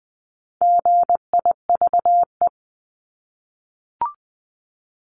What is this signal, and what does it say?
Morse code, heard as a keyed tone, about 20 wpm, 700 Hz. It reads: GIVE
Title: Multi-letter Phonemes Repeated 3x 20wpm